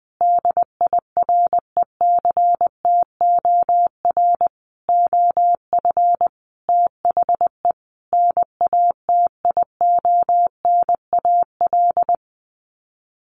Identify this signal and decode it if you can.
Morse code, heard as a keyed tone, about 20 wpm, 700 Hz. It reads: DIRECTOR OF THE NATIONAL
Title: Sets of 5 Words Encore2 20wpm